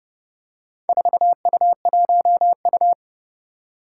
Morse code, heard as a keyed tone, about 30 wpm, 700 Hz. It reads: 4U1U
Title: Four Challenging Letter-Number Combinations 30wpm